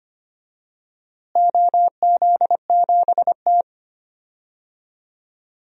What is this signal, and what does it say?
Morse code, heard as a keyed tone, about 25 wpm, 700 Hz. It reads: OZ7T
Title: DXCC-100 Call Signs - 4 Characters - Repeated 3x - 25wpm